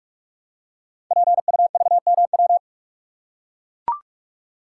Morse code, heard as a keyed tone, about 44 wpm, 700 Hz. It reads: PUVMW
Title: Code Groups - 5 Letters - 45wpm